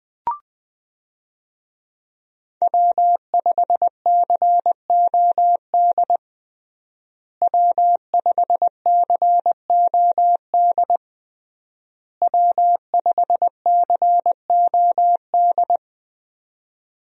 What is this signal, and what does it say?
Morse code, heard as a keyed tone, about 20 wpm, 700 Hz. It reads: W5COD W5COD W5COD
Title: USA Call Signs - 5 Characters - Repeated 3x - 20wpm